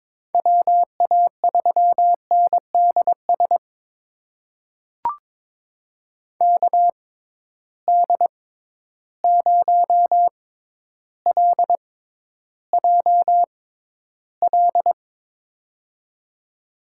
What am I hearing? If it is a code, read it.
Morse code, heard as a keyed tone, about 22 wpm, spaced as for 11 wpm, 700 Hz. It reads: WA3NDS KD0LJL
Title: USA Call Signs - 6 Characters - 6x Character Spacing - 22wpm